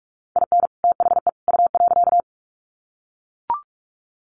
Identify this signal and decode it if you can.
Morse code, heard as a keyed tone, about 44 wpm, 700 Hz. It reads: IN THE U.
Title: Sets of 3 Words 45wpm